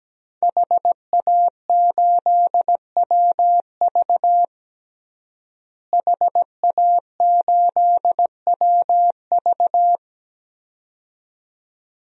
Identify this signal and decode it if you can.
Morse code, heard as a keyed tone, about 17 wpm, 700 Hz. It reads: HA8WV HA8WV
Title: DXCC-100 Call Signs - 5 Characters - Repeated 2x - 17wpm